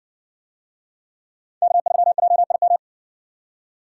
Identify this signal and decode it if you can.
Morse code, heard as a keyed tone, about 56 wpm, 700 Hz. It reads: K3JIM